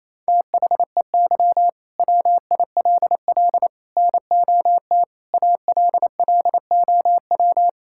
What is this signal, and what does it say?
Morse code, heard as a keyed tone, about 27 wpm, 700 Hz. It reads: THEY WILL NOT ALLOW
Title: Sentences from Top 1000 Words - Review Part 5 - 28wpm